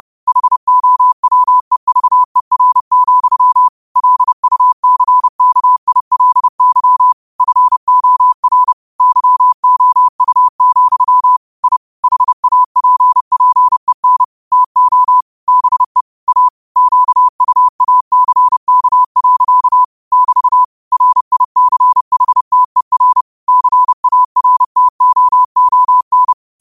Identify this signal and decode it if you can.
Morse code, heard as a keyed tone, about 30 wpm, 1000 Hz. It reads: HOWEVER, LUCKILY FOR YOU, I HAPPEN TO BE A QUACK. <BT> RICHTER CARTOON
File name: 30wpm